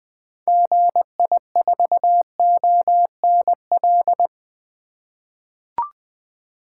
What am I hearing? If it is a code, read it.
Morse code, heard as a keyed tone, about 20 wpm, 700 Hz. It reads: GI4ONL
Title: 2023 SOTA Activator Call Signs - 20wpm